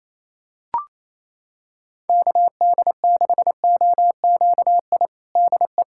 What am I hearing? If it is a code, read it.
Morse code, heard as a keyed tone, about 27 wpm, 700 Hz. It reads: KD6OQI DE
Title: USA Callsign DE Callsign - 6 Characters - 28wpm